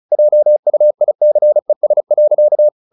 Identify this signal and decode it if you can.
Morse code, heard as a keyed tone, about 34 wpm, 600 Hz. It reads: JUICES.
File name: the old man and the sea / the old man and the sea 1 35wpm